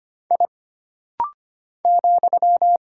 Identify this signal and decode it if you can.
Morse code, heard as a keyed tone, about 25 wpm, 700 Hz. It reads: I ,
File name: Single Character (Letter:Number:Punctuation:Prosign) - Rapid-Fire - 25wpm